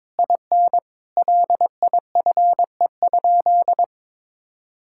Morse code, heard as a keyed tone, about 22 wpm, 700 Hz. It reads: IN LIFE?
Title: Sentences from Top 1600 Words 22wpm